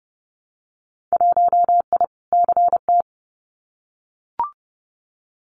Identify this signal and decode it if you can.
Morse code, heard as a keyed tone, about 30 wpm, 700 Hz. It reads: 1I CT
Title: Winter Field Day Exchanges 30wpm